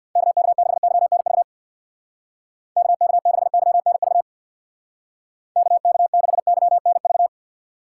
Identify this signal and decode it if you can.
Morse code, heard as a keyed tone, about 63 wpm, 700 Hz. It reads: KK6YNV KK6YNV KK6YNV